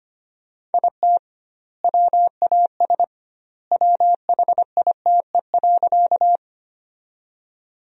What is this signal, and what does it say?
Morse code, heard as a keyed tone, about 25 wpm, 700 Hz. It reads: IT WAS WHITE.
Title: 2x Word-Spacing - Sentences from Top 300 Words - 25wpm